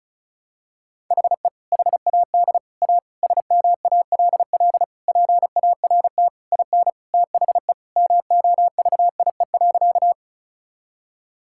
Morse code, heard as a keyed tone, about 36 wpm, 700 Hz. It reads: HE HAD A SMALL PART IN THE MOVIE.
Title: Sentences from Top 1000 Words - Review Part 5 - 35wpm